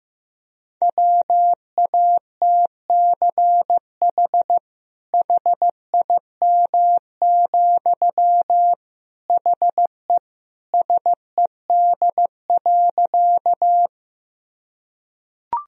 Morse code, heard as a keyed tone, about 15 wpm, 700 Hz. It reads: WATCH HIM, HE SED.
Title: Sentences from Top 300 Words and Abbreviations 15wpm